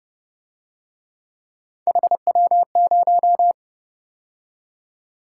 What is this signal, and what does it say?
Morse code, heard as a keyed tone, about 31 wpm, 700 Hz. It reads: HW0
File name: Speed-Racing - Three Letter-Number Combinations - 25wpm